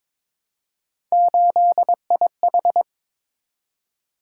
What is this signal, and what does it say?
Morse code, heard as a keyed tone, about 21 wpm, 700 Hz. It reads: 8IH
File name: practice / 22wpm / Three Letter-Number Combinations - 22wpm